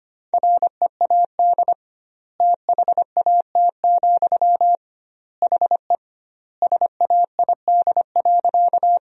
Morse code, heard as a keyed tone, about 25 wpm, 700 Hz. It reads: READ THAT, HE SAID.
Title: 2x Word-Spacing - Sentences from Top 200 Words - 25wpm